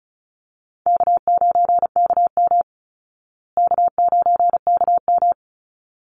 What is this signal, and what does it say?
Morse code, heard as a keyed tone, about 34 wpm, 700 Hz. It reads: K9KM K9KM
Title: Call Signs Repeated 3x - 35wpm